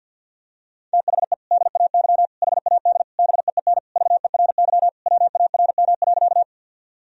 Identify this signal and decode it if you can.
Morse code, heard as a keyed tone, about 47 wpm, 700 Hz. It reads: THE DAY HAD BEEN VERY WARM.
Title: Sentences from Top 1000 Words - Review Part 2 - 50wpm